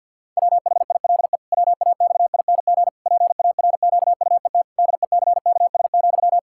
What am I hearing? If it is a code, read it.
Morse code, heard as a keyed tone, about 52 wpm, 700 Hz. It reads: WHILE WAXING PARQUET DECKS,
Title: Pangram Challenge v2 - 50wpm